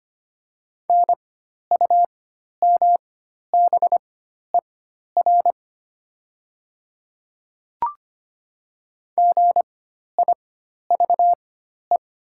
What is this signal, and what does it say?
Morse code, heard as a keyed tone, about 25 wpm, spaced as for 12 wpm, 700 Hz. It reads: NUMBER GIVE
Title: 4x Character Spacing - Top 200 Words - 25wpm